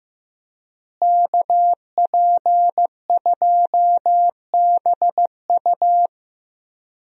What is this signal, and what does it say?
Morse code, heard as a keyed tone, about 15 wpm, 700 Hz. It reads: KP2BU
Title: USA Call Signs - 5 Characters - Repeated 3x - 15wpm